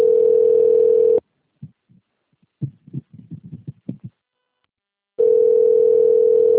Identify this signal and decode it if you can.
Ringback tone (North America), 440 + 480 Hz.